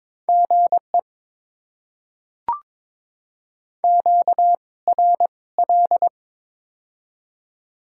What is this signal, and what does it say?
Morse code, heard as a keyed tone, about 22 wpm, spaced as for 17 wpm, 690 Hz. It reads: GE QRL
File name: Top 20 QSO Elements - 2x Character Spacing - 22wpm